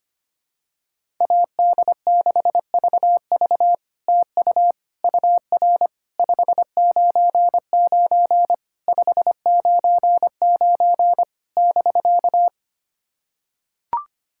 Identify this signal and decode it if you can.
Morse code, heard as a keyed tone, about 25 wpm, 700 Hz. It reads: AD6VV TU UR 599 599 <BK>
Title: SOTA QSO Practice 25wpm